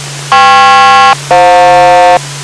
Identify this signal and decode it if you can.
Aircraft SELCAL, heard as two pairs of tones. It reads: KP-FJ